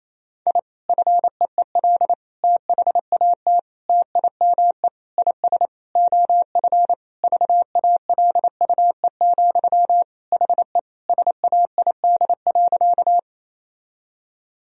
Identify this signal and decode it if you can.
Morse code, heard as a keyed tone, about 27 wpm, 700 Hz. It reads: I FEEL THAT TIME IS OF VALUE, HE SAID.
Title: Sentences from Top 1000 Words - Review Part 3 - 28wpm